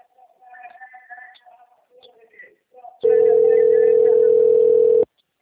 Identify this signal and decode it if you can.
Ringback tone (North America), 440 + 480 Hz.
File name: out- - - - - .7290